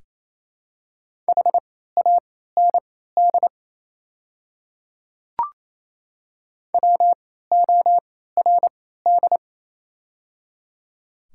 Morse code, heard as a keyed tone, about 28 wpm, spaced as for 16 wpm, 700 Hz. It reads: HAND WORD
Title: 3x Character Spacing - Top 200 Words - 28wpm